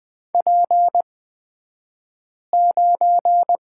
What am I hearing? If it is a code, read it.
Morse code, heard as a keyed tone, about 20 wpm, spaced as for 5 wpm, 700 Hz. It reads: P9
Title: 249 Lesson 40d - BK - Two Characters 20wpm